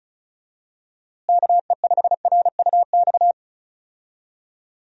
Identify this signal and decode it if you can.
Morse code, heard as a keyed tone, about 36 wpm, 700 Hz. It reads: KE5RUX
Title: USA Call Signs - 6 Characters - Repeated 2x - 35wpm